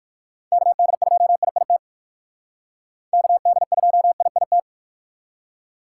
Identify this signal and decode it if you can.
Morse code, heard as a keyed tone, about 58 wpm, 700 Hz. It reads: KD2IIT KD2IIT